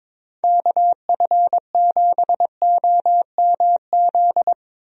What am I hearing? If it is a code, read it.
Morse code, heard as a keyed tone, about 22 wpm, 700 Hz. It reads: KF7OMZ
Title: USA Call Signs - 6 Characters - Repeated 2x - 22wpm